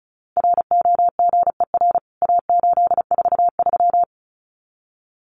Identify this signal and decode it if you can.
Morse code, heard as a keyed tone, about 34 wpm, 700 Hz. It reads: ROGER A843
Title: CWT Exchange 35wpm